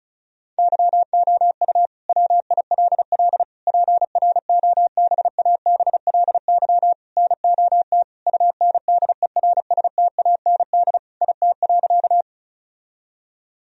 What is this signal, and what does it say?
Morse code, heard as a keyed tone, about 34 wpm, 700 Hz. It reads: YOU WILL PROBABLY NOT UNDERSTAND IT.
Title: Sentences from Top 1000 Words - Review Part 4 - 35wpm